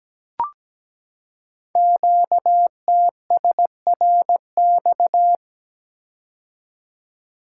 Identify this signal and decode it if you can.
Morse code, heard as a keyed tone, about 17 wpm, 700 Hz. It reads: QTSRX
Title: Code Groups - 5 Letters - 17wpm